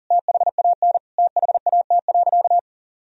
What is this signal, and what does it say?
Morse code, heard as a keyed tone, about 40 wpm, 700 Hz. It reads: THAN THAT.
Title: Sentences from Top 1000 Words - Review Part 3 - 40wpm